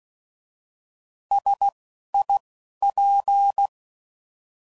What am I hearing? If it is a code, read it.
Morse code, heard as a keyed tone, about 16 wpm, spaced as for 12 wpm, 800 Hz. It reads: SIP